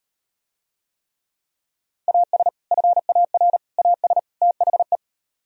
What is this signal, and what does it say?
Morse code, heard as a keyed tone, about 38 wpm, 700 Hz. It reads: AS FAR AS THE